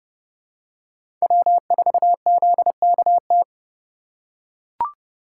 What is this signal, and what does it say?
Morse code, heard as a keyed tone, about 30 wpm, 700 Hz. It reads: W4ZKT
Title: USA Call Signs - 5 Characters - 2x Character Spacing - 30wpm